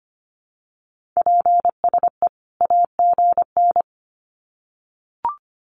Morse code, heard as a keyed tone, about 25 wpm, 700 Hz. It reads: PSE AGN
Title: Common QSO Phrases 25wpm